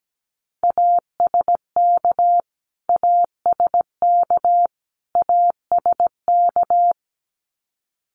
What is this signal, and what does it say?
Morse code, heard as a keyed tone, about 17 wpm, 700 Hz. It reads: ASK ASK ASK
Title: Top 300 Words Repeated 17wpm